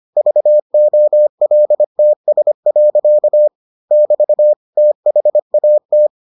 Morse code, heard as a keyed tone, about 25 wpm, 600 Hz. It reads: VOLTS. <BT> THAT